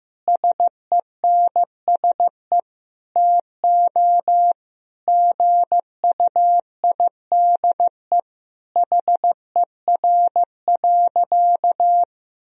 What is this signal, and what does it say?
Morse code, heard as a keyed tone, about 15 wpm, 700 Hz. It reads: SENSE TO GUIDE HER.
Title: Sentences from Top 1000 Words - Review Part 4 - 15wpm